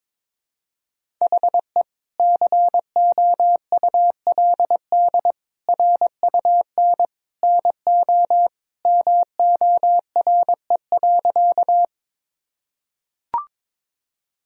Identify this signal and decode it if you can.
Morse code, heard as a keyed tone, about 22 wpm, 700 Hz. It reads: HE COULD RUN NO MORE.
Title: Sentences from Top 1000 Words - Review Part 1 - 22wpm